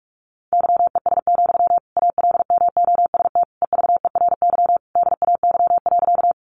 Morse code, heard as a keyed tone, about 44 wpm, 700 Hz. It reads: YES, ALMOST EVERY DAY.
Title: Sentences from Top 700 Words 45wpm